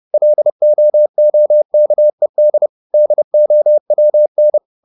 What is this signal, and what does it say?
Morse code, heard as a keyed tone, about 30 wpm, 600 Hz. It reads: LOOKED DOWN
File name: the old man and the sea / the old man and the sea 1 30wpm